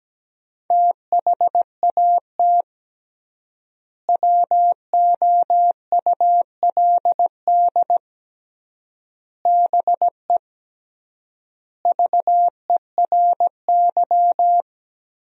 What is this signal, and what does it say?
Morse code, heard as a keyed tone, about 17 wpm, 700 Hz. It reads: THAT WOULD BE VERY
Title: 3x Word-Spacing - Sentences from Top 200 Words - 17wpm